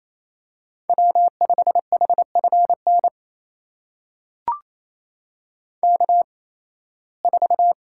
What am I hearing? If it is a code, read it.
Morse code, heard as a keyed tone, about 28 wpm, spaced as for 18 wpm, 700 Hz. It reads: W5HFN K4
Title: USA Call Signs - 5 Characters - 8x Character Spacing - 28wpm